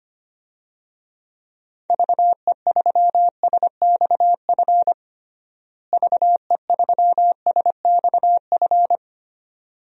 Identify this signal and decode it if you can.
Morse code, heard as a keyed tone, about 25 wpm, 700 Hz. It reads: VE3SXF VE3SXF
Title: DXCC-100 Call Signs - 6 Characters - Repeated 2x - 25wpm